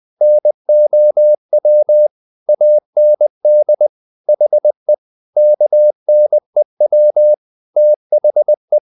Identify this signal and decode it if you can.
Morse code, heard as a keyed tone, about 20 wpm, 600 Hz. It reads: NOW AND HE KNEW THE
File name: the old man and the sea / the old man and the sea part 1 20wpm